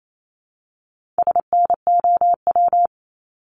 Morse code, heard as a keyed tone, about 28 wpm, 700 Hz. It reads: SNOW
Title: Multi-letter Phonemes Repeated 3x 28wpm